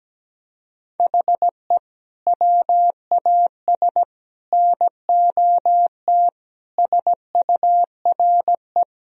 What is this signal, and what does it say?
Morse code, heard as a keyed tone, about 17 wpm, 700 Hz. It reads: HE WAS NOT SURE
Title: Sentences from Top 1000 Words - Review Part 1 - 17wpm